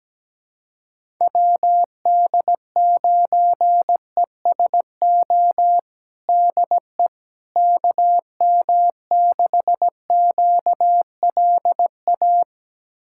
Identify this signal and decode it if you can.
Morse code, heard as a keyed tone, about 17 wpm, 700 Hz. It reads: WD9ESO DE KM6QLA
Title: USA Callsign DE Callsign 17wpm